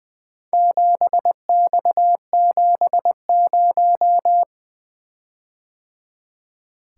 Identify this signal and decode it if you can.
Morse code, heard as a keyed tone, about 20 wpm, 700 Hz. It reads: 7X70
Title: Four Letter-Number Combinations - 20wpm